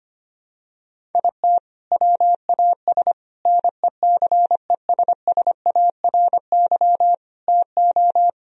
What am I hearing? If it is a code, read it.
Morse code, heard as a keyed tone, about 25 wpm, 700 Hz. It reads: IT WAS NECESSARY TO
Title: Sentences from Top 1000 Words - Review Part 1 - 25wpm